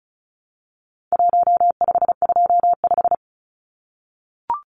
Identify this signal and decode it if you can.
Morse code, heard as a keyed tone, about 34 wpm, 700 Hz. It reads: 1525
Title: Numbers - Contest Style - 4-Digit 35wpm